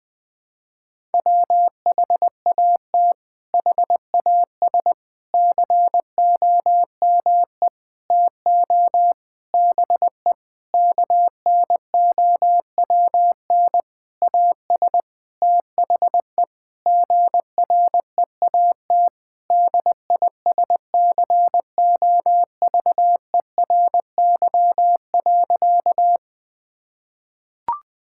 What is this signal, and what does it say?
Morse code, heard as a keyed tone, about 20 wpm, 700 Hz. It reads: WHAT HAS COME TO BE KNOWN AS THE GREAT DISCOVERY.
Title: Sentences with 8-Word Phrases - Spoken Beginning - 20wpm